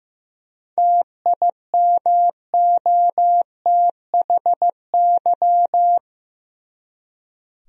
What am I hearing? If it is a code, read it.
Morse code, heard as a keyed tone, about 15 wpm, 700 Hz. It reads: TIMOTHY
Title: USA Common Names - 15wpm